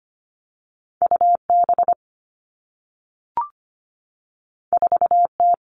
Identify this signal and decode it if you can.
Morse code, heard as a keyed tone, about 25 wpm, 700 Hz. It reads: UB 4T
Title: Two Challenging Letter-Number Combinations 25wpm